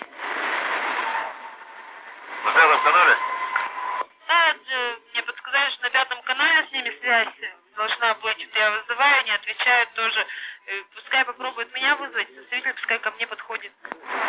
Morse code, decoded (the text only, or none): none